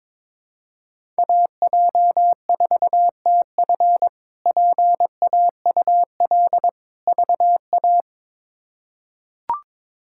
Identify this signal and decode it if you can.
Morse code, heard as a keyed tone, about 22 wpm, 700 Hz. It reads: AJ4TF PAUL VA
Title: K1USN SST QSO Practice - 22wpm